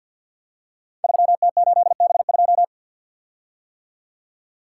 Morse code, heard as a keyed tone, about 48 wpm, 700 Hz. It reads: 3T8B2
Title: Code Groups - 5 Letter:Numbers - 50wpm